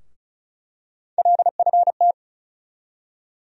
Morse code, heard as a keyed tone, about 36 wpm, 700 Hz. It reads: LFT
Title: USA Call Sign Postfixes - 3 Letters 35wpm